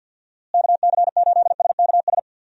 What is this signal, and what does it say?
Morse code, heard as a keyed tone, about 51 wpm, 700 Hz. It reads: KC8SKS
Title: USA Call Signs - 6 Characters - Repeated 3x - 50wpm